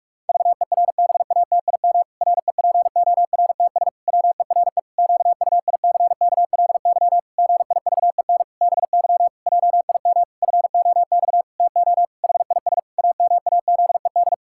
Morse code, heard as a keyed tone, about 46 wpm, 700 Hz. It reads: VERBATIM REPORTS WERE QUICKLY GIVEN BY JIM FOX TO HIS AMAZED
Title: Pangram Challenge v2 - 45wpm